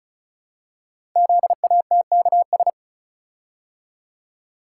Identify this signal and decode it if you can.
Morse code, heard as a keyed tone, about 36 wpm, 700 Hz. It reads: ZATKS
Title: Code Groups - 5 Letters - 35wpm